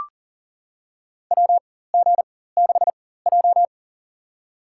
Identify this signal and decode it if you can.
Morse code, heard as a keyed tone, about 40 wpm, spaced as for 19 wpm, 700 Hz. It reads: WG6J
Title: USA Call Signs - 4 Characters - 4x Character Spacing - 40wpm